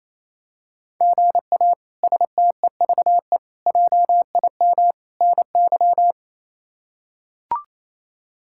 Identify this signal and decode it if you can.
Morse code, heard as a keyed tone, about 28 wpm, 700 Hz. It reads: GA STEVE JIM NY
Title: K1USN SST QSO Practice - 28wpm